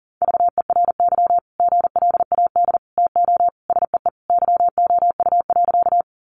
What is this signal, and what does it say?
Morse code, heard as a keyed tone, about 40 wpm, 700 Hz. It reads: VERY GLAD TO SEE YOU.
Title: Sentences from Top 800 Words 40wpm